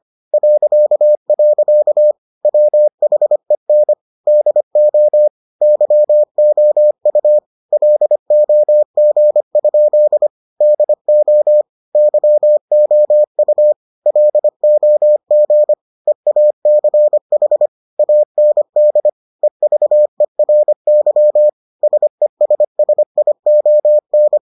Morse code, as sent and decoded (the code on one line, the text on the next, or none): .-.-.- .-.-.- / .-- .... -. / -.. --- / -.-- --- ..- / .-.. --- --. ..--.. / -.. --- / -.-- --- ..- / .-.. --- --. / . .- -.-. .... / .- -. -.. / . ...- . .-. -.-- / ... ... ... .. --- -.
.. WHEN DO YOU LOG? DO YOU LOG EACH AND EVERY SESSION